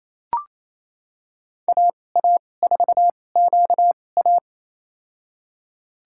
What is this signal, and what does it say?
Morse code, heard as a keyed tone, about 28 wpm, spaced as for 20 wpm, 700 Hz. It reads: AA4QA
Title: USA Call Signs - 5 Characters - 2x Character Spacing - 28wpm